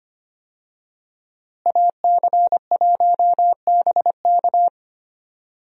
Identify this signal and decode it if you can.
Morse code, heard as a keyed tone, about 25 wpm, 700 Hz. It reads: AC1BK
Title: USA Call Signs - 5 Characters - Repeated 3x - 25wpm